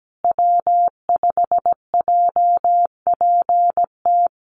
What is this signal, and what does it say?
Morse code, heard as a keyed tone, about 17 wpm, 700 Hz. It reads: W5JPT
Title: USA Call Signs Repeated 2x 17wpm